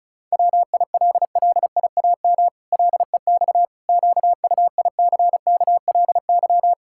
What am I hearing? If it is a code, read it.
Morse code, heard as a keyed tone, about 36 wpm, 700 Hz. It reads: WILLIAM LEX QUICKLY
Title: Pangram Challenge v2 - 35wpm